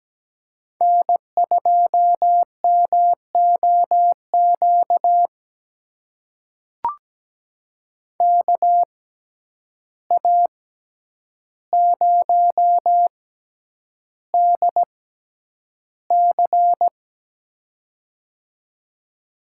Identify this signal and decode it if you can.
Morse code, heard as a keyed tone, about 17 wpm, spaced as for 9 wpm, 700 Hz. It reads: N2MOQ KA0DC
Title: USA Call Signs - 5 Characters - 6x Character Spacing - 17wpm